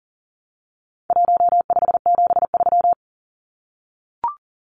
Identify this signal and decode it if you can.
Morse code, heard as a keyed tone, about 40 wpm, 700 Hz. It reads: 1573
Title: Numbers - Contest Style - 4-Digit 40wpm